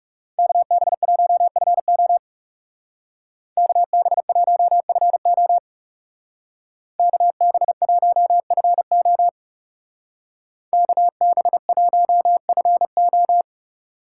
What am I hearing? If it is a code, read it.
Morse code, heard as a keyed tone, about 37 wpm, 700 Hz. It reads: KB1FO KB1FO KB1FO KB1FO
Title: Speed Racing - USA 2x2 Call Signs 30wpm